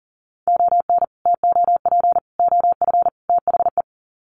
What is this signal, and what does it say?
Morse code, heard as a keyed tone, about 40 wpm, 700 Hz. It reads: ON TOP OF THE
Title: Sets of 4 Words 40wpm